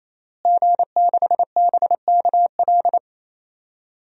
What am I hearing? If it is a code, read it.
Morse code, heard as a keyed tone, about 28 wpm, 700 Hz. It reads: G6BKL
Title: DXCC-100 Call Signs - 5 Characters - Repeated 3x - 28wpm